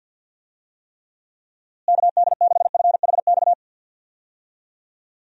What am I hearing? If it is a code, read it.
Morse code, heard as a keyed tone, about 50 wpm, 700 Hz. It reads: KD6FHX
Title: USA Call Signs - 6 Characters - Repeated 2x - 50wpm